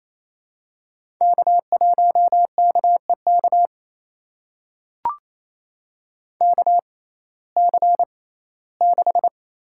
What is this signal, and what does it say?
Morse code, heard as a keyed tone, about 28 wpm, spaced as for 17 wpm, 700 Hz. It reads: K1KEK KC6
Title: USA Call Signs - 6x Character Spacing - 28wpm